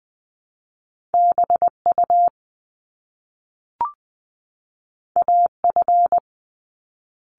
Morse code, heard as a keyed tone, about 20 wpm, 700 Hz. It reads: BU AF
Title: Top Two Letter Combinations in Words 20wpm